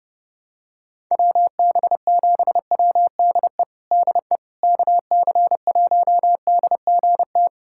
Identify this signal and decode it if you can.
Morse code, heard as a keyed tone, about 32 wpm, 700 Hz. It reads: WB7WDE DE KC1DGT
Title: USA Callsign DE Callsign - 6 Characters - 30wpm